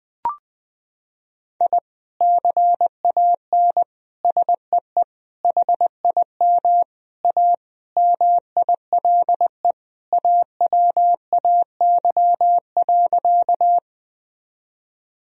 Morse code, heard as a keyed tone, about 20 wpm, 700 Hz. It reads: I CAN SEE HIM A MILE AWAY.
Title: Sentences from Top 1000 Words - Review Part 1 - 20wpm